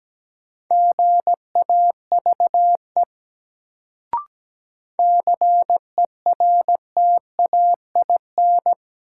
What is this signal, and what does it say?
Morse code, heard as a keyed tone, about 17 wpm, 700 Hz. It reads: GAVE CERTAIN
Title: Sentences from Top 500 Words - New Word Drill - 17wpm